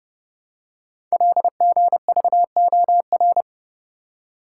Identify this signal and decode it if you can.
Morse code, heard as a keyed tone, about 29 wpm, 700 Hz. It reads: LGVOR
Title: Code Groups - 5 Letters - 30wpm